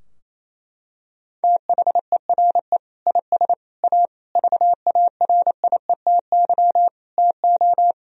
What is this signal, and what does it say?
Morse code, heard as a keyed tone, about 28 wpm, 700 Hz. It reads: THERE IS A VARIETY TO
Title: Sentences from Top 1000 Words - Review Part 4 - 28wpm